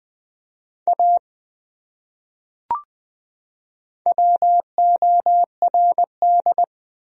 Morse code, heard as a keyed tone, about 20 wpm, 700 Hz. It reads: A WORD
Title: Top 100 Words 20wpm